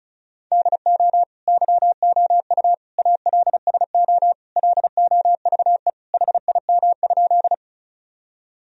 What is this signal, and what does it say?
Morse code, heard as a keyed tone, about 36 wpm, 700 Hz. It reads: DO YOU ALSO LOVE HIM?
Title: Sentences from Top 1000 Words - Review Part 1 - 35wpm